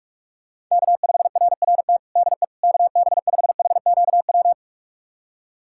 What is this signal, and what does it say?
Morse code, heard as a keyed tone, about 41 wpm, 700 Hz. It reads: K5RRT DE KB5HQW
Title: USA Callsign DE Callsign 45wpm